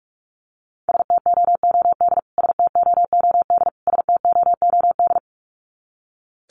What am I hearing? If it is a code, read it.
Morse code, heard as a keyed tone, about 44 wpm, 700 Hz. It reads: STOOD STOOD STOOD
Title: Top 500 Words Repeated 45wpm